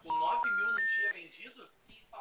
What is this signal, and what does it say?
Special information tone, with speech.